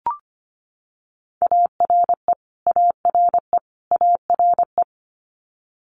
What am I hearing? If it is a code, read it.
Morse code, heard as a keyed tone, about 25 wpm, 700 Hz. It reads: ARE ARE ARE
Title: Multi-letter Phonemes Repeated 3x 25wpm